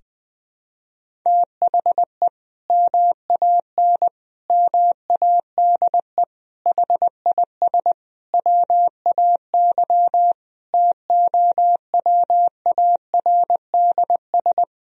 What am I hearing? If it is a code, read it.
Morse code, heard as a keyed tone, about 21 wpm, 700 Hz. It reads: THE MAN MADE HIS WAY TOWARDS
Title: Sentences from Top 1000 Words - Review Part 2 - 20wpm